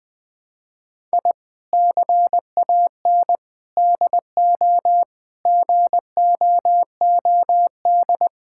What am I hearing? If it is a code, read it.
Morse code, heard as a keyed tone, about 20 wpm, 700 Hz. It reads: I CAN DO GOOD
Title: Sentences from Top 1000 Words - Review Part 6 - 20wpm